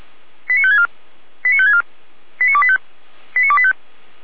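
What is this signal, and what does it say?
Five-tone selective call (ZVEI), heard as quick runs of five tones, tones of about 70 ms. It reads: 89564 89564 89285 89285
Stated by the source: ZVEI five-tone selective call received on SDR.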